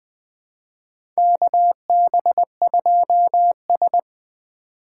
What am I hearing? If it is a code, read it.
Morse code, heard as a keyed tone, about 20 wpm, 700 Hz. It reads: KB2S
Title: K1USN SST QSO Practice - 20wpm